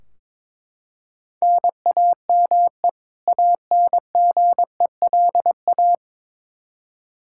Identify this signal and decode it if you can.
Morse code, heard as a keyed tone, about 22 wpm, 700 Hz. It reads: NAME ANGELA
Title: Practice QSO - Exchange 1 - Beginning - 22wpm